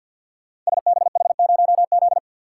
Morse code, heard as a keyed tone, about 52 wpm, 690 Hz. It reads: SBH0Z
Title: Code Groups - 5 Letter:Numbers - 50wpm